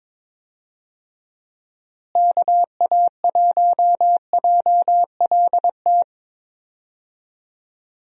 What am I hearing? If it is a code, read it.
Morse code, heard as a keyed tone, about 22 wpm, 700 Hz. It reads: KA1JLT
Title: USA Call Signs - 6 Characters - Repeated 2x - 22wpm